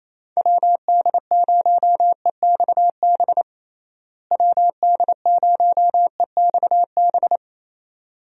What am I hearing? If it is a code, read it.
Morse code, heard as a keyed tone, about 28 wpm, 700 Hz. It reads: WD0EXB WD0EXB
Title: USA Call Signs - 6 Characters - Repeated 2x - 28wpm